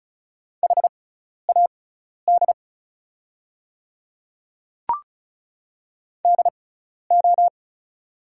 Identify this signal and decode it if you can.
Morse code, heard as a keyed tone, about 36 wpm, spaced as for 12 wpm, 700 Hz. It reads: HAD DO
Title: 6x Character Spacing - Top 100 Words - 35wpm